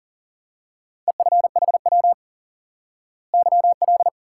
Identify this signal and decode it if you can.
Morse code, heard as a keyed tone, about 40 wpm, 700 Hz. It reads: EFHW YL
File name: IQE Course 3 - Lesson 89 - YAGI - Two Elements - 40wpm